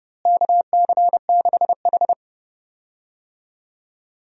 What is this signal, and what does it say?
Morse code, heard as a keyed tone, about 30 wpm, 700 Hz. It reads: KC6H
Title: USA Call Signs - 4 Characters - Repeated 2x - 30wpm